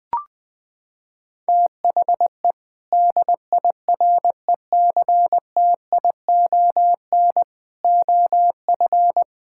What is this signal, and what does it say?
Morse code, heard as a keyed tone, about 20 wpm, 700 Hz. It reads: THE DIRECTION OF
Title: Sentences from Top 1000 Words - Review Part 5 - 20wpm